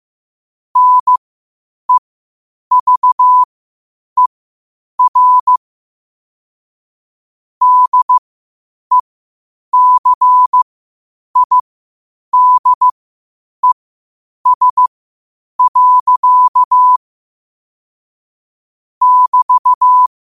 Morse code, heard as a keyed tone, about 15 wpm, spaced as for 8 wpm, 1000 Hz. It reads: NEVER DECIDES. <BT>
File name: qotdfiles / 8wpm